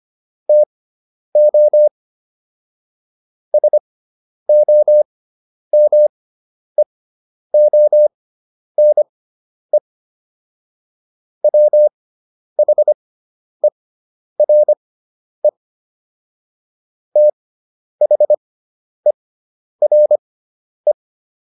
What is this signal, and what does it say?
Morse code, heard as a keyed tone, about 25 wpm, spaced as for 10 wpm, 600 Hz. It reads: TO SOMEONE WHERE THERE